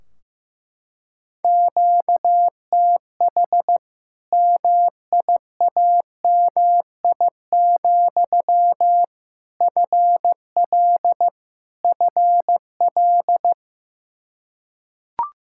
Morse code, heard as a keyed tone, about 15 wpm, 700 Hz. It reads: QTH MIAMI, FL FL
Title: Practice QSO - Exchange 1 - Beginning - 15wpm